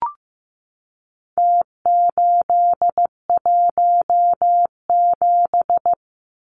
Morse code, heard as a keyed tone, about 15 wpm, 700 Hz. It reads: T817
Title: Numbers - Contest Style - 4-Digit 15wpm